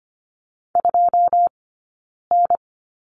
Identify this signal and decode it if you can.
Morse code, heard as a keyed tone, about 25 wpm, spaced as for 9 wpm, 690 Hz. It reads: 2N
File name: course / 25:10wpm / 092 Lesson 16d - Letter c - Two Characters 25wpm